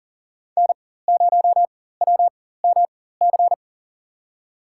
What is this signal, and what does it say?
Morse code, heard as a keyed tone, about 40 wpm, spaced as for 19 wpm, 700 Hz. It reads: N0WMC
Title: USA Call Signs - 5 Characters - 4x Character Spacing - 40wpm